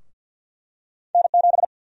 Morse code, heard as a keyed tone, about 52 wpm, 700 Hz. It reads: N7
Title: USA Call Sign Prefixes - 50wpm